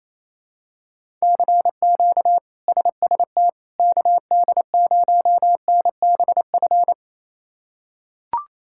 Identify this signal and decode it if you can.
Morse code, heard as a keyed tone, about 27 wpm, 700 Hz. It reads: CQ SST KD0NBF
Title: K1USN SST QSO Practice - 28wpm